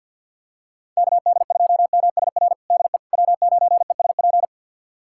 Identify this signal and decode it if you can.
Morse code, heard as a keyed tone, about 46 wpm, 700 Hz. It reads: KD2MSR DE W9ESP
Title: USA Callsign DE Callsign 50wpm